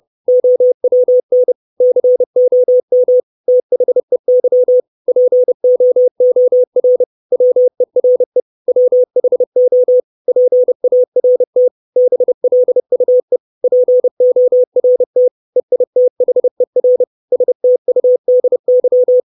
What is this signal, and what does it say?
Morse code, heard as a keyed tone, about 30 wpm, 500 Hz. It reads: OWN COM THEY POOR WERE WHO PART BLUE PORT EITHER STUDY
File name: 632 Words 30WPM